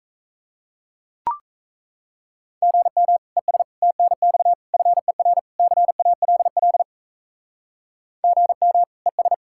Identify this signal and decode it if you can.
Morse code, heard as a keyed tone, about 34 wpm, 700 Hz. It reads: GM ES TNX FER CALL GM ES
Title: Speed-Racing Common QSO Phrases Encore - 28wpm